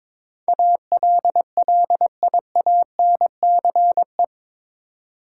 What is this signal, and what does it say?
Morse code, heard as a keyed tone, about 22 wpm, 700 Hz. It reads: ALLIANCE
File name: practice / 22wpm / Eight Letter Words - 22wpm